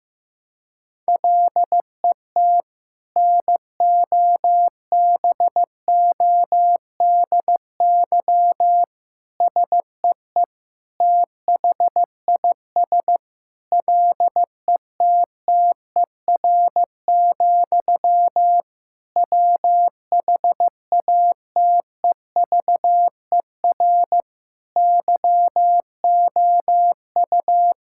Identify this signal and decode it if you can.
Morse code, heard as a keyed tone, about 15 wpm, 700 Hz. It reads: LET NOBODY SEE THIS LETTER, WHATEVER YOU
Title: Sentences from Top 1600 Words 15wpm